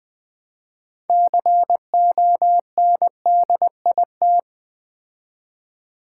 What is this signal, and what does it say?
Morse code, heard as a keyed tone, about 20 wpm, 700 Hz. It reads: CONDIT
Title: Build A Word - 3 Letters At A Time - 9-Letter Words - 20wpm